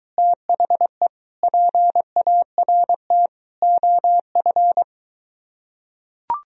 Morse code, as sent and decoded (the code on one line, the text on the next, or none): - .... / .--. .- .-. - / --- ..-.
THE PART OF